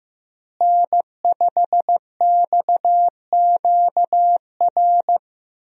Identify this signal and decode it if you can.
Morse code, heard as a keyed tone, about 15 wpm, 700 Hz. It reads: N5XQR
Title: USA Call Signs - 5 Characters - Repeated 2x - 15wpm